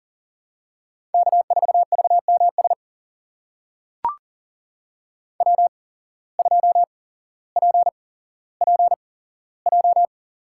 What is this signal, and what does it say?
Morse code, heard as a keyed tone, about 40 wpm, spaced as for 17 wpm, 700 Hz. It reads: K4VMS W2PPJ
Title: USA Call Signs - 5 Characters - 8x Character Spacing - 40wpm